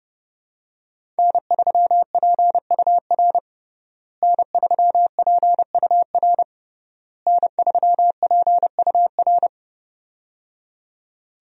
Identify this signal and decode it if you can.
Morse code, heard as a keyed tone, about 29 wpm, 700 Hz. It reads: N3PUR N3PUR N3PUR
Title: USA Call Signs - 5 Characters - Repeated 3x - 30wpm